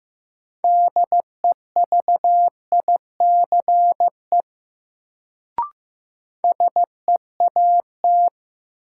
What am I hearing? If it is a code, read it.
Morse code, heard as a keyed tone, about 15 wpm, 700 Hz. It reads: DEVICE SEAT
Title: Sentences from Top 1500 Words - New Word Drill Alt - 15wpm